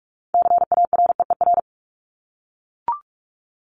Morse code, heard as a keyed tone, about 44 wpm, 700 Hz. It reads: CAREER
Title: Top 751-1000 Words 45wpm